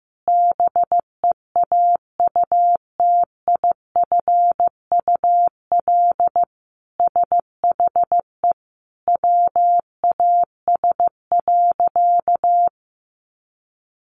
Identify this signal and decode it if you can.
Morse code, heard as a keyed tone, about 15 wpm, 700 Hz. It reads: BEAUTIFUL SHE WAS.
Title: Sentences from Top 900 Words 15wpm